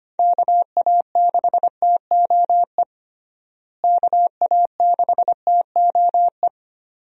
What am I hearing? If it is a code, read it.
Morse code, heard as a keyed tone, about 25 wpm, 700 Hz. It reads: KA6TOE KA6TOE
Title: USA Call Signs - 6 Characters - Repeated 2x - 25wpm